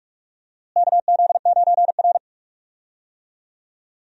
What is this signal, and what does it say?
Morse code, heard as a keyed tone, about 46 wpm, 700 Hz. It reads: KZ9R
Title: USA Call Signs - 4 Characters - Repeated 3x - 45wpm